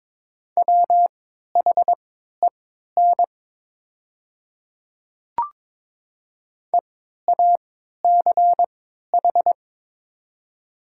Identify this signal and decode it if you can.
Morse code, heard as a keyed tone, about 23 wpm, spaced as for 13 wpm, 700 Hz. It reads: WHEN EACH
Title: 3x Character Spacing - Top 200 Words - 22wpm